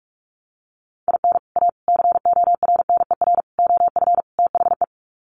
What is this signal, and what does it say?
Morse code, heard as a keyed tone, about 44 wpm, 700 Hz. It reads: IN A CORNER OF THE
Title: Sets of 5 Words Encore2 45wpm